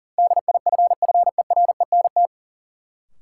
Morse code, heard as a keyed tone, about 38 wpm, 700 Hz. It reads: DIFFERENT
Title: Sentences from Top 600 Words - New Word Drill - 40wpm